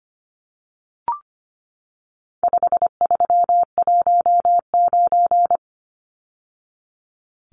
Morse code, heard as a keyed tone, about 25 wpm, 700 Hz. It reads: 5319
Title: Numbers - 4-Digit 25wpm